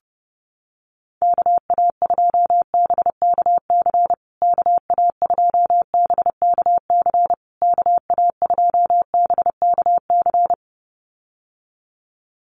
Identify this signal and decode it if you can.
Morse code, heard as a keyed tone, about 30 wpm, 700 Hz. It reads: KA2BKC KA2BKC KA2BKC
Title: USA Call Signs Repeated 3x 30wpm